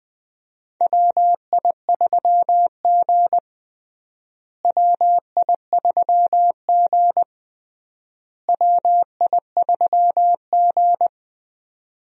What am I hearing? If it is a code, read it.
Morse code, heard as a keyed tone, about 20 wpm, 700 Hz. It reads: WI3G WI3G WI3G
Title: USA Call Signs - 4 Characters - Repeated 3x - 20wpm